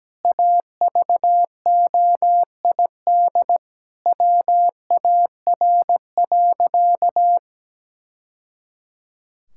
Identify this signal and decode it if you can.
Morse code, heard as a keyed tone, about 17 wpm, 700 Hz. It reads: AVOID WAR.
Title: Sentences from Top 1000 Words - Review Part 6 - 17wpm